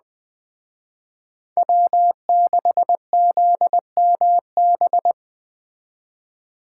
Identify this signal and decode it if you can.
Morse code, heard as a keyed tone, about 20 wpm, 700 Hz. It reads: W6ZMB